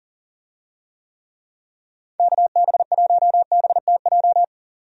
Morse code, heard as a keyed tone, about 39 wpm, 690 Hz. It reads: KB1BTJ
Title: USA Call Signs - 6 Characters - 40wpm